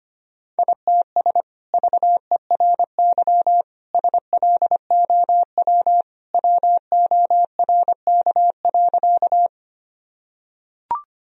Morse code, heard as a keyed tone, about 25 wpm, 700 Hz. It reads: ITS VERY SLOW WORK.
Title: Sentences from Top 1000 Words - Review Part 2 - 25wpm